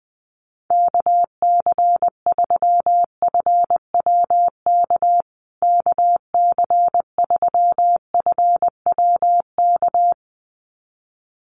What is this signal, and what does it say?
Morse code, heard as a keyed tone, about 20 wpm, 700 Hz. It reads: KC3FWK KC3FWK
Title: USA Call Signs Repeated 2x 20wpm